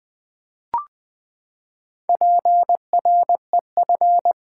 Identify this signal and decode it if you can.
Morse code, heard as a keyed tone, about 20 wpm, 700 Hz. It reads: PREF
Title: Build A Word - 4 Letters At A Time - 10-Letter Words - 20wpm